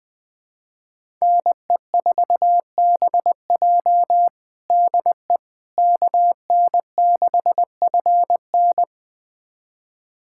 Morse code, heard as a keyed tone, about 20 wpm, 700 Hz. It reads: NE4BJ DE KN6FN
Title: USA Callsign DE Callsign - 5 Characters - 20wpm